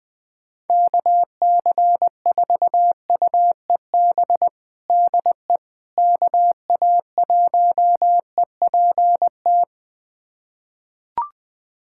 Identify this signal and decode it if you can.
Morse code, heard as a keyed tone, about 20 wpm, 700 Hz. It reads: KC4UEB DE KA1EPT
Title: USA Callsign DE Callsign - 6 Characters - 20wpm